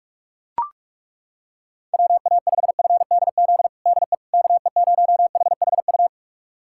Morse code, heard as a keyed tone, about 41 wpm, 700 Hz. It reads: WA5FDZ DE KE0HHU
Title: USA Callsign DE Callsign 45wpm